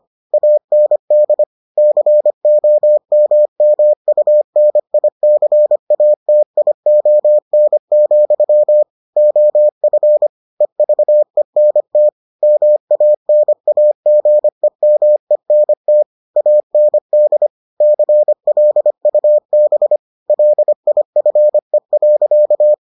Morse code, heard as a keyed tone, about 25 wpm, 600 Hz. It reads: AND COMMUNICATION, OF EVENT MANAGEMENT AND CLUB LIFE.